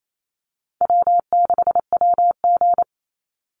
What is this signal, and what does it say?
Morse code, heard as a keyed tone, about 28 wpm, 700 Hz. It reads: W6WG
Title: Easy Call Signs - 28wpm